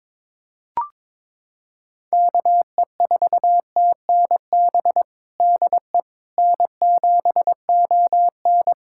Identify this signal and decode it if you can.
Morse code, heard as a keyed tone, about 22 wpm, 700 Hz. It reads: KE4TNB DE N7ON
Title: Practice QSO - Exchange 4 - Ending - 22wpm